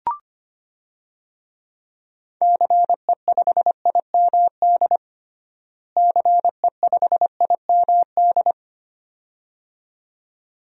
Morse code, heard as a keyed tone, about 25 wpm, 700 Hz. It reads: CE5IMD CE5IMD
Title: DXCC-100 Call Signs - 6 Characters - Repeated 2x - 25wpm